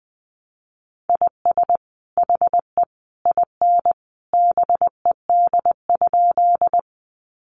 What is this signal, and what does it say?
Morse code, heard as a keyed tone, about 19 wpm, 700 Hz. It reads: IS HE IN BED?
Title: Sentences from Top 500 Words 20wpm